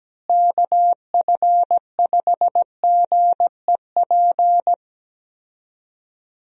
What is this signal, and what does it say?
Morse code, heard as a keyed tone, about 17 wpm, 700 Hz. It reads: KF5GEP
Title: USA Call Signs - 6 Characters - Repeated 2x - 17wpm